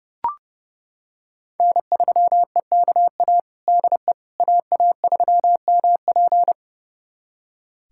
Morse code, heard as a keyed tone, about 30 wpm, 700 Hz. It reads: N3EKA DE AA3MP
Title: USA Callsign DE Callsign - 5 Characters - 30wpm